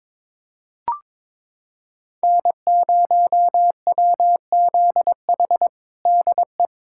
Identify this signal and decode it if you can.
Morse code, heard as a keyed tone, about 22 wpm, 700 Hz. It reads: N0WZH DE
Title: USA Callsign DE Callsign - 5 Characters - 22wpm